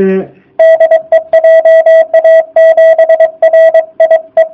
Morse code, heard as a keyed tone, about 22 wpm, 660 Hz. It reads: DE JA7RIE
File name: jh8cbh / 2019cw / sp24- w